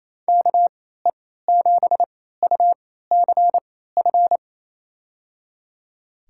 Morse code, heard as a keyed tone, about 28 wpm, spaced as for 16 wpm, 700 Hz. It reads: KE7UCF
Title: USA Call Signs - 3x Character Spacing - 28wpm